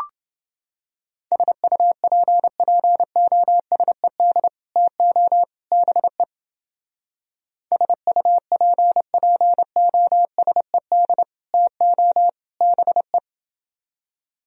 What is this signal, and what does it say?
Morse code, heard as a keyed tone, about 29 wpm, 700 Hz. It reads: SUPPOSED TO BE SUPPOSED TO BE